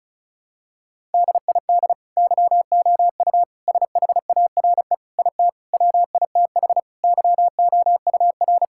Morse code, heard as a keyed tone, about 36 wpm, 700 Hz. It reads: DID YOU SHARE IT WITH YOUR
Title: Sentences from Top 1000 Words - Review Part 6 - 35wpm